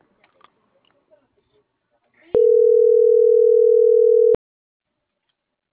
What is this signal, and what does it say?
Ringback tone (North America), 440 + 480 Hz.
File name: CENTRAL GOMSA / out francisco.cobos